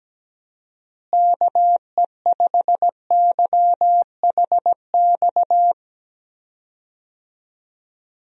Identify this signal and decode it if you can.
Morse code, heard as a keyed tone, about 17 wpm, 700 Hz. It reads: KE5YHX
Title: USA Call Signs - 6 Characters - Repeated 3x - 17wpm